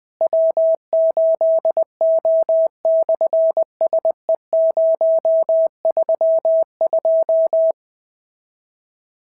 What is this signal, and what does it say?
Morse code, heard as a keyed tone, about 20 wpm, 690 Hz. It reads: W8O/SE032
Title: SOTA QSO Practice 20wpm